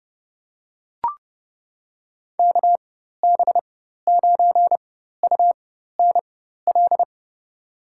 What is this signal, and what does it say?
Morse code, heard as a keyed tone, about 29 wpm, spaced as for 14 wpm, 700 Hz. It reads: KB9UNL
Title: USA Call Signs - 6 Characters - 4x Character Spacing - 30wpm